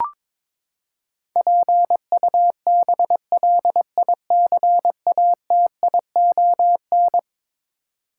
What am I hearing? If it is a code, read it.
Morse code, heard as a keyed tone, about 22 wpm, 700 Hz. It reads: PUBLICATION
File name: practice / 22wpm / Eleven Letter Words - 22wpm